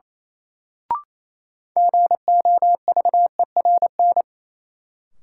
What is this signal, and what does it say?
Morse code, heard as a keyed tone, about 27 wpm, 700 Hz. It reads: GOVERN
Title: Sentences from Top 1000 Words - Review Part 2 - New Word Drill - 28wpm